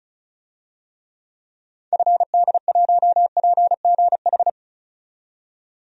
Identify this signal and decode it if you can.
Morse code, heard as a keyed tone, about 36 wpm, 700 Hz. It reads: FD1PGH
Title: DXCC-100 Call Signs - 6 Characters - 35wpm